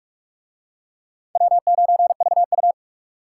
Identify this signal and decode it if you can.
Morse code, heard as a keyed tone, about 37 wpm, 700 Hz. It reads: W9VU
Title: Speed Racing - USA 1x2 Call Signs 30wpm